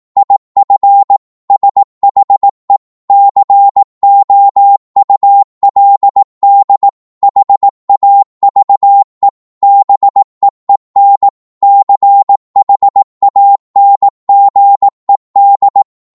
Morse code, heard as a keyed tone, about 18 wpm, 800 Hz. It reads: IF SHE COULD HAVE BEEN CHANGED
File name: Morse / psg-18wpm